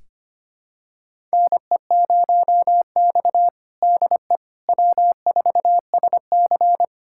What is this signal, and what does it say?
Morse code, heard as a keyed tone, about 25 wpm, 700 Hz. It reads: NE0X DE W4SC
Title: USA Callsign DE Callsign - 4 Characters - 25wpm